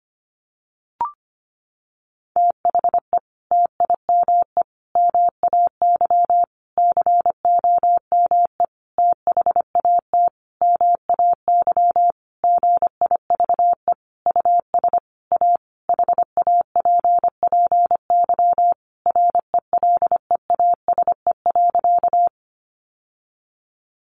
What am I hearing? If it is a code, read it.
Morse code, heard as a keyed tone, about 25 wpm, 700 Hz. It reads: THE TIME MAY COME THAT MAY GIVE US A HAPPY RELEASE.
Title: Sentences from Top 1100 Words 25wpm